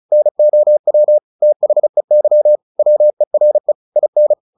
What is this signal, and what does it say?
Morse code, heard as a keyed tone, about 34 wpm, 600 Hz. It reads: NOW THEY WERE IN
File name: the old man and the sea / the old man and the sea 1 35wpm